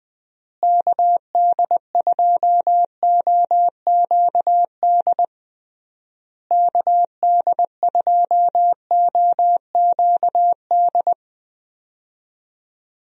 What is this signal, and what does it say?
Morse code, heard as a keyed tone, about 20 wpm, 700 Hz. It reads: KD2OQD KD2OQD
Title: USA Call Signs - 6 Characters - Repeated 3x - 20wpm